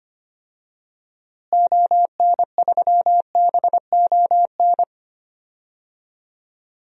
Morse code, heard as a keyed tone, about 25 wpm, 700 Hz. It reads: ON3BON
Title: DXCC-100 Call Signs - 6 Characters - Repeated 2x - 25wpm